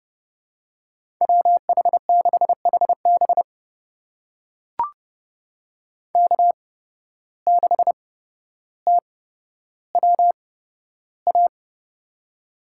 Morse code, heard as a keyed tone, about 29 wpm, spaced as for 13 wpm, 700 Hz. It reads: WH6HB K6TWA
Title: USA Call Signs - 5 Characters - 8x Character Spacing - 30wpm